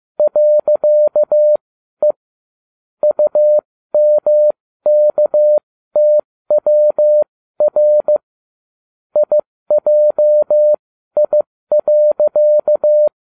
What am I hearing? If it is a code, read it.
Morse code, heard as a keyed tone, about 15 wpm, spaced as for 13 wpm, 600 Hz. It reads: .E UMKTWR IJI.